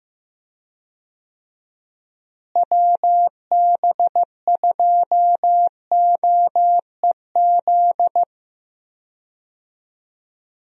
Morse code, heard as a keyed tone, about 15 wpm, 700 Hz. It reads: WB2OEZ
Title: USA Call Signs - 6 Characters - Repeated 2x - 15wpm